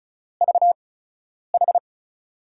Morse code, heard as a keyed tone, about 36 wpm, spaced as for 10 wpm, 700 Hz. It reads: VH
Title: USA Call Signs - 8x Character Spacing - 35wpm